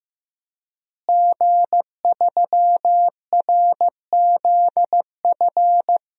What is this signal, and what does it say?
Morse code, heard as a keyed tone, about 16 wpm, 700 Hz. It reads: G3RZF
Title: DXCC-100 Call Signs - 5 Characters - Repeated 2x - 15wpm